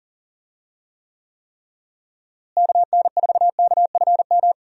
Morse code, heard as a keyed tone, about 40 wpm, 700 Hz. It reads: KN4KFM
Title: USA Call Signs - 6 Characters - Repeated 3x - 40wpm